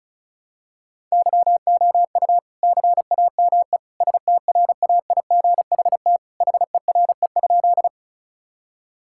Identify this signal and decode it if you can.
Morse code, heard as a keyed tone, about 36 wpm, 700 Hz. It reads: YOU CAME STRAIGHT HERE?
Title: Sentences from Top 1000 Words - Review Part 6 - 35wpm